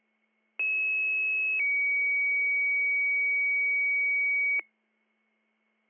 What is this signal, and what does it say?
Two-tone page, heard as a one-second tone, then a three-second tone, about 2600 then 2250 Hz. It is heard from 0.6 to 4.6 s.